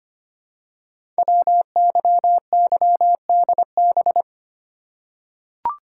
Morse code, heard as a keyed tone, about 25 wpm, 700 Hz. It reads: WYYDB
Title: Code Groups - 5 Letters - 25wpm